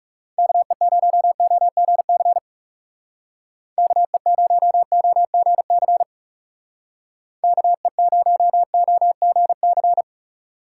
Morse code, heard as a keyed tone, about 37 wpm, 700 Hz. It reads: KE0OGC KE0OGC KE0OGC
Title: Speed Racing - USA 2x3 Call Signs 30wpm